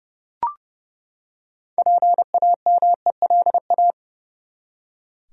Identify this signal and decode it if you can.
Morse code, heard as a keyed tone, about 30 wpm, 700 Hz. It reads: PAMELA
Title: USA Common Names - 30wpm